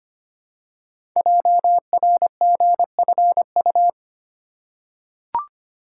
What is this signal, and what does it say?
Morse code, heard as a keyed tone, about 25 wpm, 700 Hz. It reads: JRGFU
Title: Code Groups - 5 Letters - 25wpm